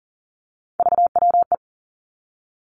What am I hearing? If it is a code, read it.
Morse code, heard as a keyed tone, about 42 wpm, 700 Hz. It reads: VWE
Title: Three Challenging Letter-Number Combinations 40wpm